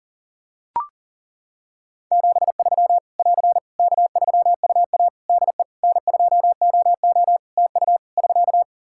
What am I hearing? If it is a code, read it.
Morse code, heard as a keyed tone, about 40 wpm, 700 Hz. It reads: 73 <AR> K3UA DE N2OO TU <SK>
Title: K1USN SST QSO Practice 40wpm